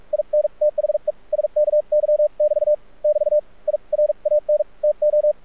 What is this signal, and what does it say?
Morse code, heard as a keyed tone, about 46 wpm, spaced as for 39 wpm, 600 Hz. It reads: IN THE SKY <BT> <BT> I RAN TO
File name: archived / CRSnet